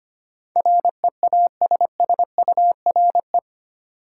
Morse code, heard as a keyed tone, about 25 wpm, 700 Hz. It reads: REASSURE
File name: 8 Letter Words - First 3 Letters Given - 25wpm